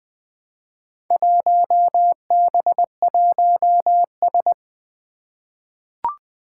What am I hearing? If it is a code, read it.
Morse code, heard as a keyed tone, about 20 wpm, 700 Hz. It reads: 1B1S
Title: Four Challenging Letter-Number Combinations 20wpm